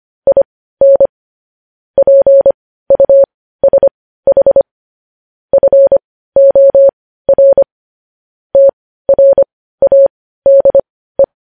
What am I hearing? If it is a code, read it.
Morse code, heard as a keyed tone, about 25 wpm, spaced as for 15 wpm, 550 Hz. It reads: IN PUSH FOR TRADE
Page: news headlines in morse code at 15 wpm